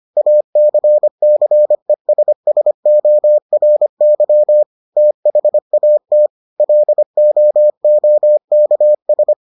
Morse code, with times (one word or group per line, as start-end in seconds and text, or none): 0.2-4.6 ACCESSORY
5.0-6.3 THAT
6.6-9.3 LOOKS